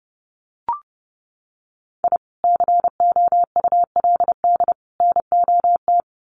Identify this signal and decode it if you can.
Morse code, heard as a keyed tone, about 30 wpm, 700 Hz. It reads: I COULD NOT
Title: Sets of 3 Words 30wpm